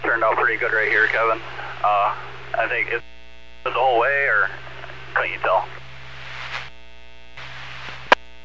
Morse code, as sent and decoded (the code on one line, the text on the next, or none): none